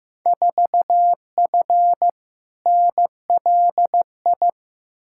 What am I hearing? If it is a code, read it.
Morse code, heard as a keyed tone, about 15 wpm, 700 Hz. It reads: 4F NLI
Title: Field Day Exchanges Encore - 15wpm